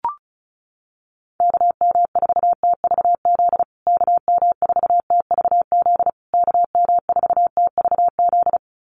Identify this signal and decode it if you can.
Morse code, heard as a keyed tone, about 34 wpm, 700 Hz. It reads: KM4TVZ KM4TVZ KM4TVZ
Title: USA Call Signs Repeated 3x 35wpm